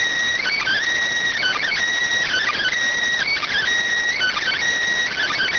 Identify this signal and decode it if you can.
Emergency Alert System (SAME) tones (header bursts), heard as harsh buzzing bursts. It is heard from 0.4 to 4.6 s.